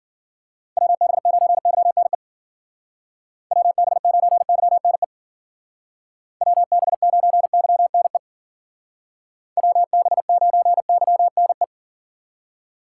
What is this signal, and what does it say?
Morse code, heard as a keyed tone, about 53 wpm, 700 Hz. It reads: WB9YNE WB9YNE WB9YNE WB9YNE